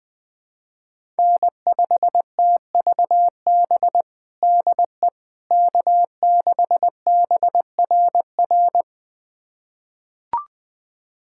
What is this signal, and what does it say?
Morse code, heard as a keyed tone, about 21 wpm, 700 Hz. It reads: N5TVB DE K6BRR
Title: USA Callsign DE Callsign - 5 Characters - 20wpm